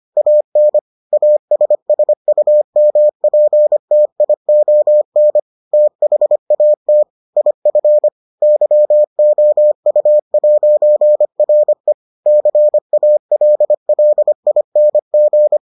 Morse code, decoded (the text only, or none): AN ASSUMPTION THAT IF YOU'RE CALLING